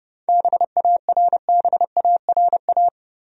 Morse code, heard as a keyed tone, about 32 wpm, 700 Hz. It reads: BARBARA
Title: USA Common Names - 30wpm